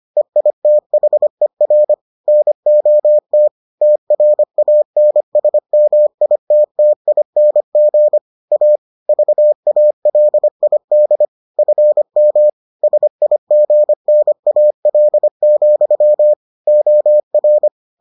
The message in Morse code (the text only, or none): EITHER NOT TRANSMITTING A VALID FM SIGNAL, OR